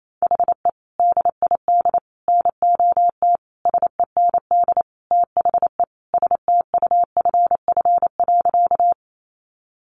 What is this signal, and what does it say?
Morse code, heard as a keyed tone, about 28 wpm, 700 Hz. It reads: HE DID NOT SEND THE STUFF.
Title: Sentences from Top 1500 Words 28wpm